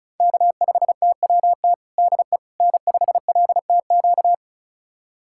Morse code, heard as a keyed tone, about 36 wpm, 700 Hz. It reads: K5TWT DE N5LTQ
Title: USA Callsign DE Callsign - 5 Characters - 35wpm